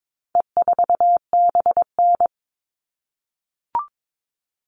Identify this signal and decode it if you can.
Morse code, heard as a keyed tone, about 22 wpm, 700 Hz. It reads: E4BN
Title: All Cut Numbers - 4-Digit 22wpm